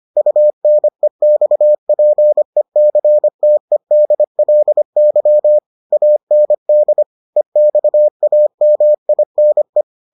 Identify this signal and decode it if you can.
Morse code, heard as a keyed tone, about 25 wpm, 600 Hz. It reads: UNEXPECTEDLY AND EXAMINE